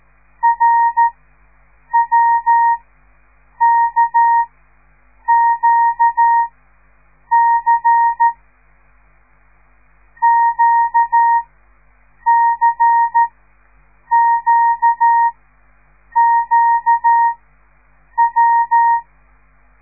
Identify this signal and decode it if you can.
Morse code, heard as a keyed tone, about 13 wpm, spaced as for 7 wpm, 930 Hz. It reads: RWKQC QCQQW
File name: radiomorningstar / morse / les 5